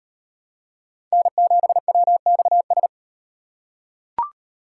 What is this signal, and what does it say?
Morse code, heard as a keyed tone, about 38 wpm, 700 Hz. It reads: N7WXS